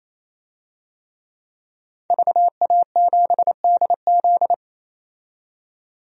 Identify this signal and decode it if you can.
Morse code, heard as a keyed tone, about 28 wpm, 700 Hz. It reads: VA7DZ
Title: DXCC-100 Call Signs - 5 Characters - 28wpm